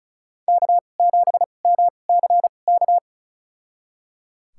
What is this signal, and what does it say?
Morse code, heard as a keyed tone, about 36 wpm, spaced as for 25 wpm, 700 Hz. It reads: K7MCK
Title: USA Call Signs - 5 Characters - 2x Character Spacing - 35wpm